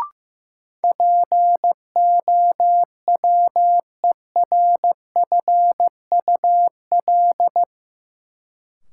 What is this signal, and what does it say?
Morse code, heard as a keyed tone, about 15 wpm, 700 Hz. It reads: POWERFUL
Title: Sentences from Top 1300 Words - New Word Drill Alt - 15wpm